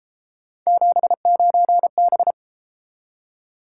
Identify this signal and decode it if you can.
Morse code, heard as a keyed tone, about 32 wpm, 700 Hz. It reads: 79B